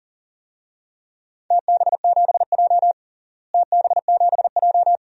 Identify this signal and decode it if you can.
Morse code, heard as a keyed tone, about 40 wpm, 700 Hz. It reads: TB7J TB7J
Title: DXCC-100 Call Signs - 4 Characters - Repeated 2x - 40wpm